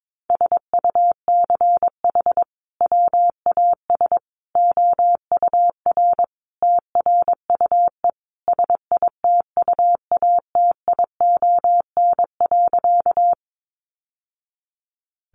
Morse code, heard as a keyed tone, about 22 wpm, 700 Hz. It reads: SUCH WAS OUR TRUE SITUATION.
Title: Sentences from Top 1200 Words 22wpm